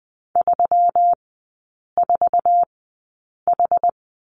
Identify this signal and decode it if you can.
Morse code, heard as a keyed tone, about 19 wpm, spaced as for 8 wpm, 700 Hz. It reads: 34H
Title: 127 Lesson 21e - Number 6 - Three Characters 20wpm